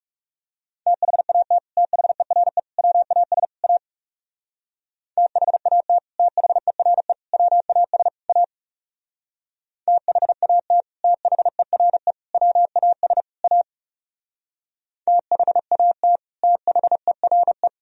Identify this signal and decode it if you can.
Morse code, heard as a keyed tone, about 35 wpm, 700 Hz. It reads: THAT THERE WAS A THAT THERE WAS A THAT THERE WAS A THAT THERE
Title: Speed-Racing - Sets of 4 Words 30wpm